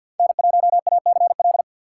Morse code, heard as a keyed tone, about 51 wpm, 700 Hz. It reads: N1ACL
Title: USA Call Signs - 5 Characters - 2x Character Spacing - 50wpm